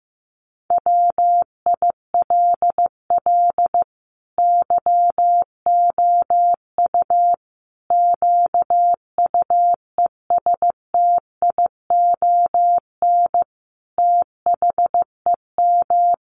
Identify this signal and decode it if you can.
Morse code, heard as a keyed tone, about 15 wpm, 700 Hz. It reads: WILL YOU QUESTION THEM
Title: Sentences from Top 1100 Words 15wpm